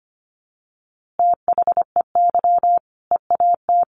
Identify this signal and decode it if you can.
Morse code, heard as a keyed tone, about 25 wpm, 700 Hz. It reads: THEY EAT
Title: Easy Phrases 25wpm